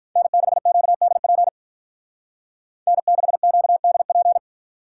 Morse code, heard as a keyed tone, about 44 wpm, 700 Hz. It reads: N6QDP N6QDP